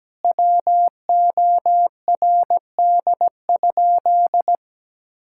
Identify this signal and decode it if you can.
Morse code, heard as a keyed tone, about 17 wpm, 700 Hz. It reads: WORD?
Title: Sentences from Top 1000 Words - Review Part 6 - 17wpm